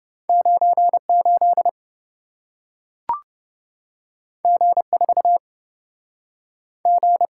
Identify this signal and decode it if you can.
Morse code, heard as a keyed tone, about 27 wpm, 700 Hz. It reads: 98 G4 G